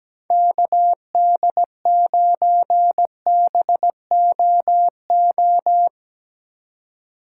Morse code, heard as a keyed tone, about 17 wpm, 700 Hz. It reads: KD9BOO
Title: USA Call Signs - 6 Characters - 4x Character Spacing - 17wpm